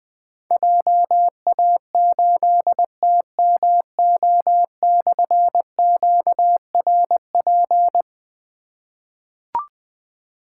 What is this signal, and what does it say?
Morse code, heard as a keyed tone, about 20 wpm, 700 Hz. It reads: JA8TMO/QRP
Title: Difficult Call Signs 20wpm